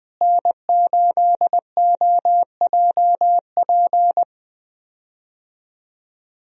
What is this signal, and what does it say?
Morse code, heard as a keyed tone, about 21 wpm, 700 Hz. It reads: N8OJP
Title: USA Call Signs - 5 Characters - Repeated 3x - 20wpm